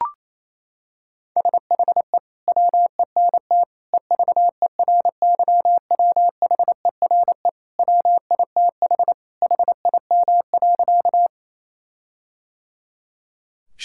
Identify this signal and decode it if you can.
Morse code, heard as a keyed tone, about 27 wpm, 700 Hz. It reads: SHE WENT EVERYWHERE WITH HIM.
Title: Sentences from Top 1600 Words 28wpm